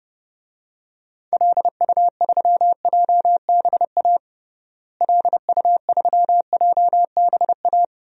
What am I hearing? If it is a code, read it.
Morse code, heard as a keyed tone, about 30 wpm, 700 Hz. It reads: LU3JBA LU3JBA
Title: DXCC-100 Call Signs - 6 Characters - Repeated 2x - 30wpm